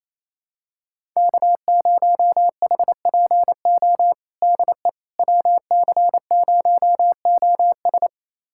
Morse code, heard as a keyed tone, about 28 wpm, 700 Hz. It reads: K0HPO DE WC0OS
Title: USA Callsign DE Callsign - 5 Characters - 28wpm